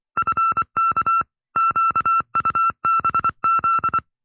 Morse code, heard as a keyed tone, about 24 wpm, 1340 Hz. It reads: FK QUBZ
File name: FreeFiles / CW-practice / 20WPM-side1